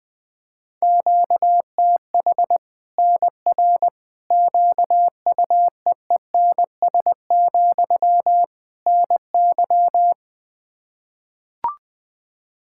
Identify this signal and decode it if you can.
Morse code, heard as a keyed tone, about 20 wpm, 700 Hz. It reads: QTH NR QUEENS, NY
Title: Practice QSO - Exchange 1 - Beginning - 20wpm